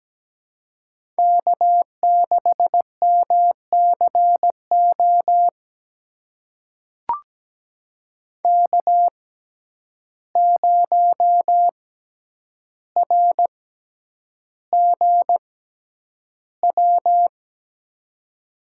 Morse code, heard as a keyed tone, about 17 wpm, spaced as for 9 wpm, 700 Hz. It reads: K6MCO K0RGW
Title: USA Call Signs - 5 Characters - 6x Character Spacing - 17wpm